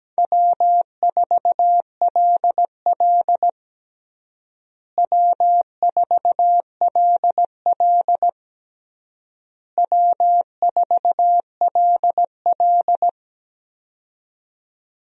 Morse code, heard as a keyed tone, about 17 wpm, 700 Hz. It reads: W4LL W4LL W4LL
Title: USA Call Signs - 4 Characters - Repeated 3x - 17wpm